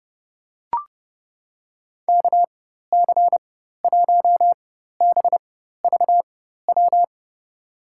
Morse code, heard as a keyed tone, about 31 wpm, spaced as for 14 wpm, 700 Hz. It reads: KC1BVW
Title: USA Call Signs - 6 Characters - 4x Character Spacing - 30wpm